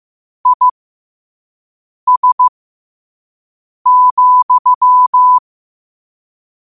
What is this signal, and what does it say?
Morse code, heard as a keyed tone, about 15 wpm, spaced as for 5 wpm, 1000 Hz. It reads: IS,
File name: qotdfiles / 5wpm